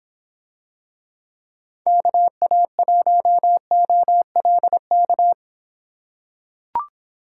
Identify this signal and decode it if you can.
Morse code, heard as a keyed tone, about 26 wpm, 700 Hz. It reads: KA1OLK